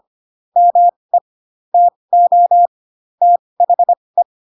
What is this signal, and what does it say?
Morse code, heard as a keyed tone, about 25 wpm, spaced as for 20 wpm, 700 Hz. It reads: ME TO THE
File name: Pinocchio / Pinocchio ch7 15 25 ik1hge